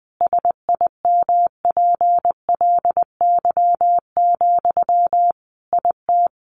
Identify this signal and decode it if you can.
Morse code, heard as a keyed tone, about 20 wpm, 700 Hz. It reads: SIMPLY, IT
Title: Sentences from Top 800 Words 20wpm